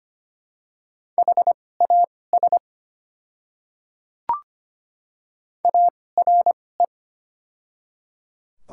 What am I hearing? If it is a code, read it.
Morse code, heard as a keyed tone, about 25 wpm, spaced as for 18 wpm, 690 Hz. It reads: HAS ARE
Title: 2x Character Spacing - Top 100 Words - 25wpm